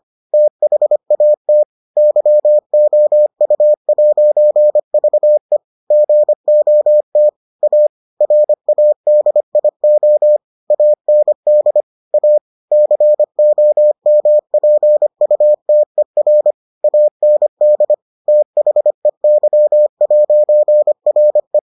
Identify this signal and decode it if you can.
Morse code, heard as a keyed tone, about 25 wpm, 600 Hz. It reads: THAT YOU'VE GOT A RADIO AND A COMPUTER AND THEY'RE